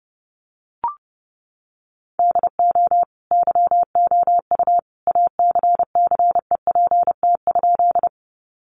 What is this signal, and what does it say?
Morse code, heard as a keyed tone, about 30 wpm, 700 Hz. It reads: DO YOU ACCEPT?
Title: Sentences from Top 1300 Words 30wpm